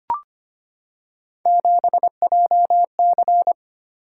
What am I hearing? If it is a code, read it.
Morse code, heard as a keyed tone, about 25 wpm, 700 Hz. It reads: 7JC
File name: Three Letter-Number Combinations - 25wpm